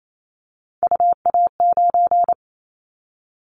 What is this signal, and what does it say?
Morse code, heard as a keyed tone, about 28 wpm, 700 Hz. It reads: UA9
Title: Callsign Prefixes for Top 25 DXCC - 28wpm